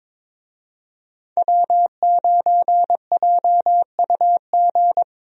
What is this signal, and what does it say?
Morse code, heard as a keyed tone, about 22 wpm, 700 Hz. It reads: W9JUG
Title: USA Call Signs - 5 Characters - Repeated 3x - 22wpm